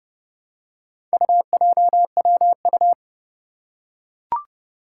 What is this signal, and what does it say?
Morse code, heard as a keyed tone, about 31 wpm, 700 Hz. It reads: UJWU
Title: Four Challenging Letter-Number Combinations 30wpm